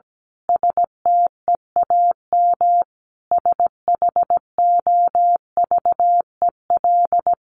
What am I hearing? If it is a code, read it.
Morse code, heard as a keyed tone, about 17 wpm, 700 Hz. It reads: STEAM SHOVEL
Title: A Sound Of Thunder 17wpm